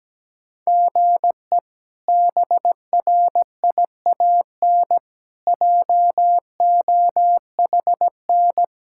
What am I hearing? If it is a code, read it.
Morse code, heard as a keyed tone, about 17 wpm, 700 Hz. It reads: GE BRIAN JOHN
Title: K1USN SST QSO Practice - 17wpm